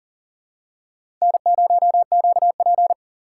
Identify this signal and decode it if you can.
Morse code, heard as a keyed tone, about 39 wpm, 700 Hz. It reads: N0QP
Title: USA Call Signs - 4 Characters - 3x Character Spacing - 40wpm